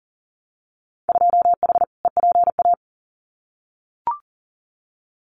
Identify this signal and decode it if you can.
Morse code, heard as a keyed tone, about 40 wpm, 700 Hz. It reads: 2H EPA
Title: Winter Field Day Exchanges Encore 40wpm